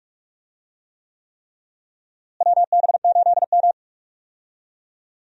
Morse code, heard as a keyed tone, about 41 wpm, 700 Hz. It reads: WB8M
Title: USA Call Signs - 4 Characters - Repeated 3x - 45wpm